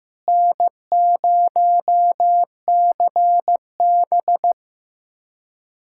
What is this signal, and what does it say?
Morse code, heard as a keyed tone, about 15 wpm, 700 Hz. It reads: N0CB
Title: USA Callsign DE Callsign - 4 Characters - 15wpm